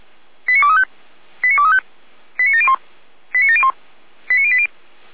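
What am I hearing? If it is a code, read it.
Five-tone selective call (ZVEI), heard as quick runs of five tones, tones of about 70 ms. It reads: 89235 89235 89701 89701 89080
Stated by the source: ZVEI five-tone selective call 89590 received by SDR.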